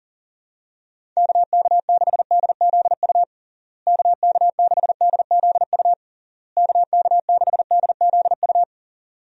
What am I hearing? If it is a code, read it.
Morse code, heard as a keyed tone, about 40 wpm, 700 Hz. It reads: KK6DZU KK6DZU KK6DZU
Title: USA Call Signs - 6 Characters - Repeated 3x - 40wpm